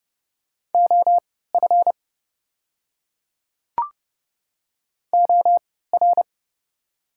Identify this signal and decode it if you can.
Morse code, heard as a keyed tone, about 30 wpm, spaced as for 17 wpm, 690 Hz. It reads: OF OR
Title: 3x Character Spacing - Top 500 Words - 30wpm